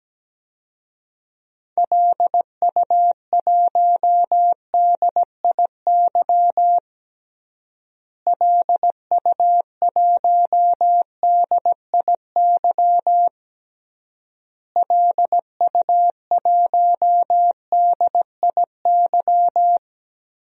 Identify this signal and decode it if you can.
Morse code, heard as a keyed tone, about 17 wpm, 700 Hz. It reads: LU1DIY LU1DIY LU1DIY
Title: DXCC-100 Call Signs - 6 Characters - Repeated 3x - 17wpm